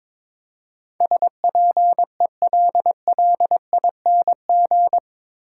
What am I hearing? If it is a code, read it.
Morse code, heard as a keyed tone, about 22 wpm, 700 Hz. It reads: SPELLING
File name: practice / 22wpm / Eight Letter Words - 22wpm